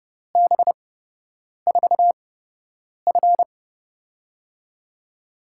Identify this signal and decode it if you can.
Morse code, heard as a keyed tone, about 30 wpm, spaced as for 8 wpm, 700 Hz. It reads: B4F
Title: USA Call Signs - 4 Characters - 8x Character Spacing - 30wpm